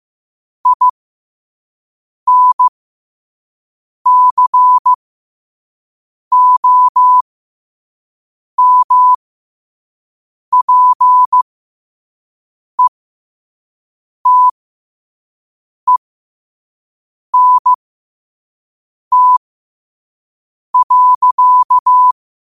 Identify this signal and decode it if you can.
Morse code, heard as a keyed tone, about 15 wpm, spaced as for 5 wpm, 1000 Hz. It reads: INCOMPETENT.
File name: qotdfiles / 5wpm